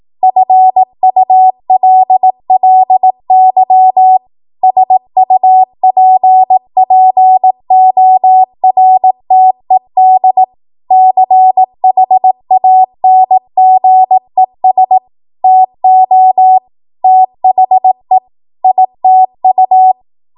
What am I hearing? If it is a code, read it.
Morse code, heard as a keyed tone, about 18 wpm, 750 Hz. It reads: FULLY SUPPORTED CHANGES TO THE ITU
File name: Morse / 18WPM